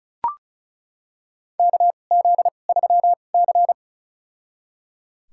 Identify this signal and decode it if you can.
Morse code, heard as a keyed tone, about 34 wpm, spaced as for 25 wpm, 700 Hz. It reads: KZ3C
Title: USA Call Signs - 2x Character Spacing - 35wpm